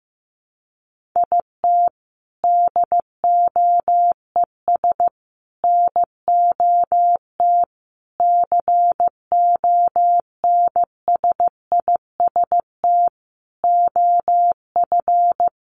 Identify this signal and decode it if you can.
Morse code, heard as a keyed tone, about 15 wpm, 700 Hz. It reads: IT DOES NOT CONSIST OF
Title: Sentences from Top 1400 Words 15wpm